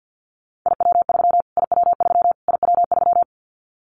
Morse code, heard as a keyed tone, about 50 wpm, 700 Hz. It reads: IW3 IW3 IW3
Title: Callsign Prefixes for Top 25 DXCC - Repeated 3x - 50wpm